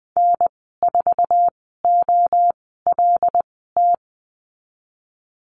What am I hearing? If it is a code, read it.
Morse code, heard as a keyed tone, about 20 wpm, spaced as for 15 wpm, 700 Hz. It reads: N4OLT
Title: 076 Lesson 13g - Letter L - Call Signs 20wpm